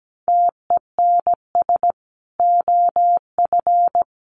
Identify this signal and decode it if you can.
Morse code, heard as a keyed tone, about 17 wpm, 700 Hz. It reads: TENS OF
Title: Sets of 5 Words Encore 17wpm